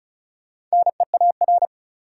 Morse code, heard as a keyed tone, about 36 wpm, 700 Hz. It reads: NEAR
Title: Sets of 5 Words Encore - New Word Drill - 35wpm